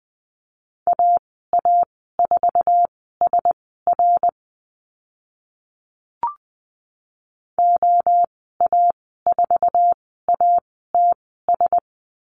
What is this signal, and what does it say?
Morse code, heard as a keyed tone, about 20 wpm, spaced as for 14 wpm, 700 Hz. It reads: AA4SR OA4ATS
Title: 062 Lesson 11g - Letter h - Call Signs 20wpm